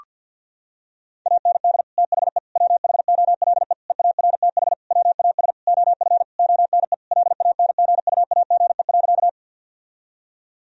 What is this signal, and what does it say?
Morse code, heard as a keyed tone, about 50 wpm, 700 Hz. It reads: AND THE WHOLE EARTH WAS OF ONE LANGUAGE.
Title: Sentences from Top 1000 Words - Review Part 3 - 50wpm